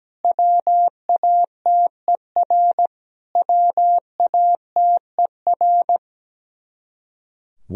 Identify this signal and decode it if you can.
Morse code, heard as a keyed tone, about 17 wpm, 700 Hz. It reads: WATER WATER
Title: Top 100 Words Repeated 17wpm